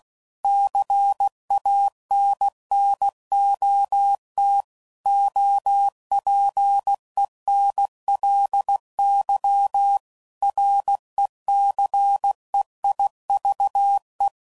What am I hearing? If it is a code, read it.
Morse code, heard as a keyed tone, about 16 wpm, 800 Hz. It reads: CANNOT OPENLY RECEIVE